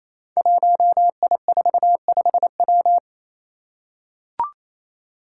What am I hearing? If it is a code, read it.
Morse code, heard as a keyed tone, about 27 wpm, 700 Hz. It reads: 1I45W
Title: Five Challenging Letter-Number Combinations 28wpm